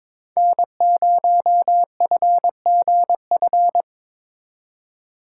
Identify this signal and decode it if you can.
Morse code, heard as a keyed tone, about 27 wpm, 700 Hz. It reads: N0FGF
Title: Speed Racing - USA 1x3 Call Signs 22wpm